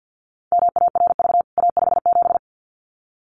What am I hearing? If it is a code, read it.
Morse code, heard as a keyed tone, about 50 wpm, 700 Hz. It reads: MARV A57
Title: CWT Exchange 50wpm